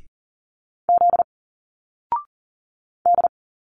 Morse code, heard as a keyed tone, about 40 wpm, 700 Hz. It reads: Z D
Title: Single Letters 40wpm